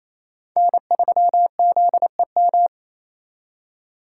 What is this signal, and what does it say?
Morse code, heard as a keyed tone, about 34 wpm, 700 Hz. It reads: N3ZEM
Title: Speed Racing - USA 1x3 Call Signs 28wpm